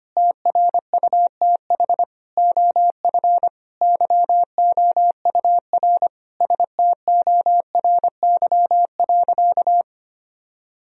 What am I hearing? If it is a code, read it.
Morse code, heard as a keyed tone, about 25 wpm, 700 Hz. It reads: TRUTH OF YOUR STORY.
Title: Sentences from Top 1000 Words - Review Part 3 - 25wpm